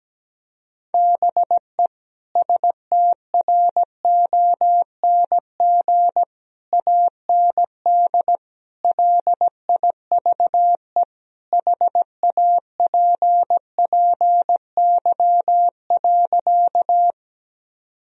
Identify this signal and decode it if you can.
Morse code, heard as a keyed tone, about 17 wpm, 700 Hz. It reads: BE STRONG AND LIVE HAPPY.
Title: Sentences from Top 1000 Words - Review Part 3 - 17wpm